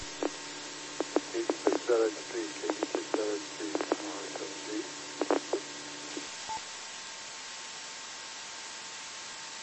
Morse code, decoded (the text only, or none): none